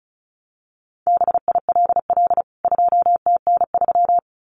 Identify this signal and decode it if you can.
Morse code, heard as a keyed tone, about 34 wpm, 700 Hz. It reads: BILL 2TN3
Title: CWT Exchange 35wpm